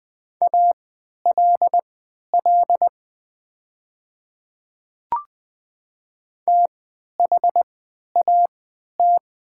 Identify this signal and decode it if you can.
Morse code, heard as a keyed tone, about 20 wpm, spaced as for 11 wpm, 700 Hz. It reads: ALL THAT
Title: 3x Character Spacing - Top 100 Words - 20wpm